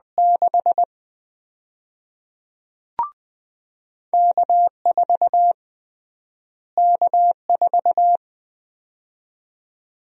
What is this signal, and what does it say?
Morse code, heard as a keyed tone, about 21 wpm, 700 Hz. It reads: 6 K4 K4
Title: USA Call Sign Prefixes - Repeated 2x - 20wpm